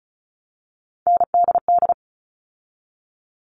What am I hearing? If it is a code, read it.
Morse code, heard as a keyed tone, about 34 wpm, 700 Hz. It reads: NDD
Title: All Cut Numbers - 3-Digit 35wpm